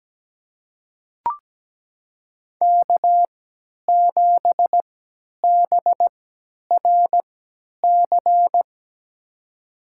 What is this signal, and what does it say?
Morse code, heard as a keyed tone, about 17 wpm, spaced as for 10 wpm, 700 Hz. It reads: K7BRC
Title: USA Call Signs - 3x Character Spacing - 17wpm